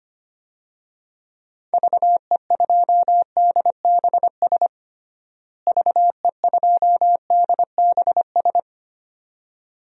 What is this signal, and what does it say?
Morse code, heard as a keyed tone, about 25 wpm, 700 Hz. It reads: VE2DBS VE2DBS
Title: DXCC-100 Call Signs - 6 Characters - Repeated 2x - 25wpm